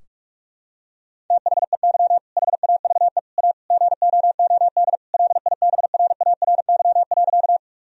Morse code, heard as a keyed tone, about 42 wpm, 700 Hz. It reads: THEY HAVE A GOOD LIBRARY.
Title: Sentences from Top 1000 Words - Review Part 4 - 45wpm